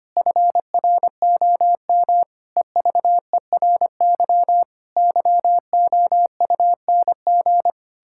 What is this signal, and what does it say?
Morse code, heard as a keyed tone, about 25 wpm, 700 Hz. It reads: FROM EVERY YOUNG
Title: Pangram Challenge v2 - 25wpm